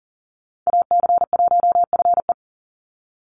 Morse code, heard as a keyed tone, about 40 wpm, 700 Hz. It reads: AC1FE
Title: Call Signs - 40wpm